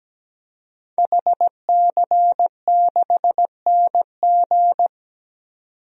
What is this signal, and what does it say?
Morse code, heard as a keyed tone, about 17 wpm, 700 Hz. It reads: HC6NG
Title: DXCC-100 Call Signs - 5 Characters - 17wpm